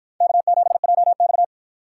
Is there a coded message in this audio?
Yes: Morse code from 0.2 to 1.4 s, heard as a keyed tone, about 49 wpm, 700 Hz. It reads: K7JX